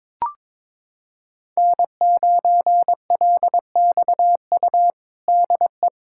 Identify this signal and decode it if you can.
Morse code, heard as a keyed tone, about 21 wpm, 700 Hz. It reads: N9LXU DE
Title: USA Callsign DE Callsign - 5 Characters - 22wpm